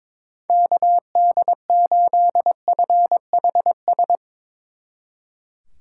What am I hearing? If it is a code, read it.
Morse code, heard as a keyed tone, about 22 wpm, 700 Hz. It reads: KD8FHS
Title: Speed Racing - USA Call Signs 22wpm